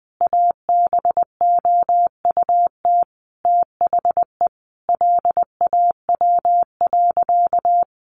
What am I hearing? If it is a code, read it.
Morse code, heard as a keyed tone, about 20 wpm, 700 Hz. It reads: ABOUT THE LAW.
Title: Sentences from Top 1400 Words 20wpm